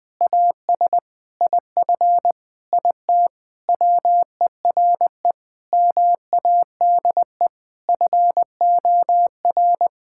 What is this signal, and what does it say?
Morse code, heard as a keyed tone, about 20 wpm, 700 Hz. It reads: AS IF IT WERE MADE FOR
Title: Sentences from Top 1600 Words 20wpm